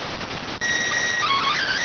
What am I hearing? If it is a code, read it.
SSTV (slow-scan television), PD 290.